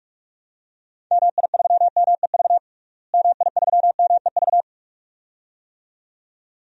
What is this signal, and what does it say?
Morse code, heard as a keyed tone, about 41 wpm, 700 Hz. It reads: MI3MEV MI3MEV
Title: DXCC-100 Call Signs - 6 Characters - Repeated 2x - 45wpm